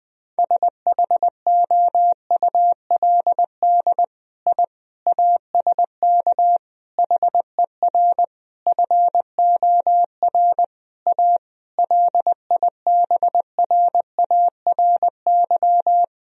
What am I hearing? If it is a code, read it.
Morse code, heard as a keyed tone, about 20 wpm, 700 Hz. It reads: SHOULD I ASK HER FOR A LIBRARY
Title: Sentences from Top 1000 Words - Review Part 4 - 20wpm